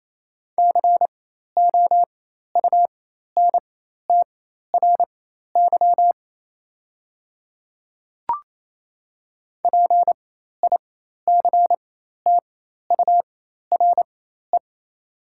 Morse code, heard as a keyed tone, about 28 wpm, spaced as for 13 wpm, 700 Hz. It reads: COUNTRY PICTURE
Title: 4x Character Spacing - Top 300 Words - 28wpm